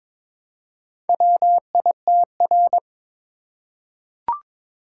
Morse code, heard as a keyed tone, about 22 wpm, 700 Hz. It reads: WITR
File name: practice / 22wpm / Four Letter-Number Combinations - 22wpm